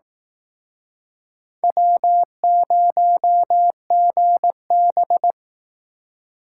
Morse code, heard as a keyed tone, about 18 wpm, 700 Hz. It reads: W0GB
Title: Speed Racing - USA Call Signs 15wpm